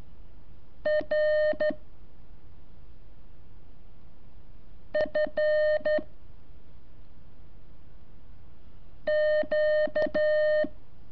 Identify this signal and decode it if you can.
Morse code, heard as a keyed tone, about 13 wpm, 610 Hz. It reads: R F Q
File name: curso / ejercicio 23